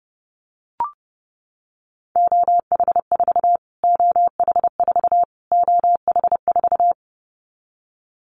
Morse code, heard as a keyed tone, about 30 wpm, 700 Hz. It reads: OH4 OH4 OH4
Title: Callsign Prefixes for Top 25 DXCC - Repeated 3x - 30wpm